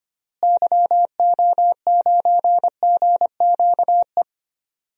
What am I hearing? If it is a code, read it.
Morse code, heard as a keyed tone, about 25 wpm, 700 Hz. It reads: YO9GQE
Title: DXCC-100 Call Signs - 6 Characters - Repeated 2x - 25wpm